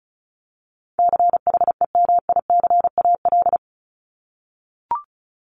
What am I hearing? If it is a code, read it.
Morse code, heard as a keyed tone, about 34 wpm, 700 Hz. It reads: CHEMICAL
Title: Top 1001-1250 Words 35wpm